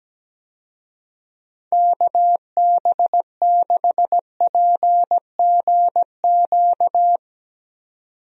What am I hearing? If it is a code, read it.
Morse code, heard as a keyed tone, about 17 wpm, 700 Hz. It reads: KB6PGQ
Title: USA Call Signs - 6 Characters - 17wpm